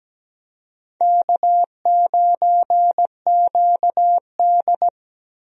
Morse code, heard as a keyed tone, about 17 wpm, 700 Hz. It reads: K9QD
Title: USA Call Signs - 4 Characters - Repeated 2x - 17wpm